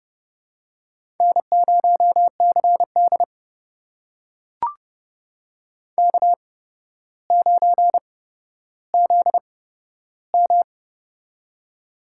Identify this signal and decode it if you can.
Morse code, heard as a keyed tone, about 31 wpm, spaced as for 13 wpm, 700 Hz. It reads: N0CD K9ZM
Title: USA Call Signs - 4 Characters - 8x Character Spacing - 30wpm